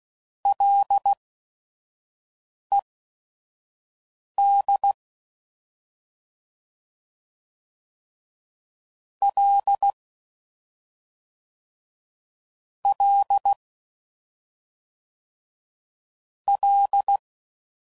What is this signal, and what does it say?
Morse code, heard as a keyed tone, about 16 wpm, spaced as for 5 wpm, 800 Hz. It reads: LED L L L
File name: Prefixes / FLs & THs / Single L's